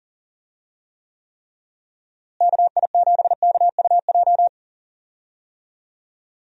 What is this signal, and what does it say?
Morse code, heard as a keyed tone, about 40 wpm, 700 Hz. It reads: KI7KUJ
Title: USA Call Signs - 6 Characters - Repeated 3x - 40wpm